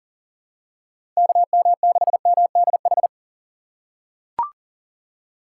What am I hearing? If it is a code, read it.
Morse code, heard as a keyed tone, about 40 wpm, 700 Hz. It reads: KM6MDH
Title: USA Call Signs - 6 Characters - 6x Character Spacing - 40wpm